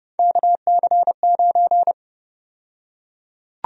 Morse code, heard as a keyed tone, about 32 wpm, 700 Hz. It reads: KC9
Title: USA Call Sign Prefixes - Repeated 2x - 30wpm